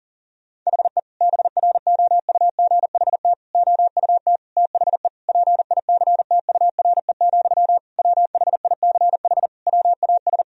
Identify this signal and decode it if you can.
Morse code, heard as a keyed tone, about 40 wpm, 700 Hz. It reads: HE BROUGHT OUT THE PICTURE, WHICH WAS
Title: Sentences from Top 1000 Words - Review Part 3 - 40wpm